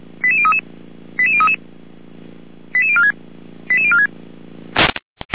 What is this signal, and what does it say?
Five-tone selective call (ZVEI), heard as quick runs of five tones, tones of about 70 ms. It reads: 88033 88033 88046 88046